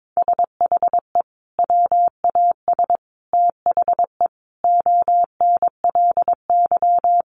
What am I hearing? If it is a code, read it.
Morse code, heard as a keyed tone, about 22 wpm, 700 Hz. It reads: SHE WAS THE ONLY
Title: Sentences from Top 1100 Words 22wpm